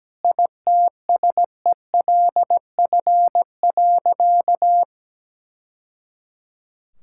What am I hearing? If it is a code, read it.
Morse code, heard as a keyed tone, about 17 wpm, 700 Hz. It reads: ITSELF.
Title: Sentences from Top 1000 Words - Review Part 5 - 17wpm